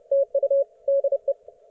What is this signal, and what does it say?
Morse code, heard as a keyed tone, about 30 wpm, 550 Hz. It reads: TU DE
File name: Archiv / CUC181 / ok1fhi cul181